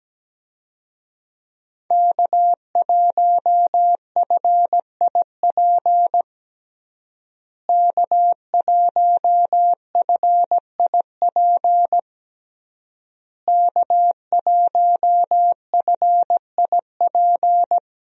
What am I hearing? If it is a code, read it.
Morse code, heard as a keyed tone, about 17 wpm, 700 Hz. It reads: K1FIP K1FIP K1FIP
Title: USA Call Signs - 5 Characters - Repeated 3x - 17wpm